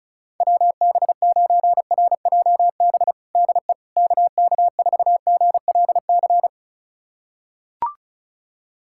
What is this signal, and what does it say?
Morse code, heard as a keyed tone, about 34 wpm, 700 Hz. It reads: WB9RJB DE KK4GLC
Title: USA Callsign DE Callsign 35wpm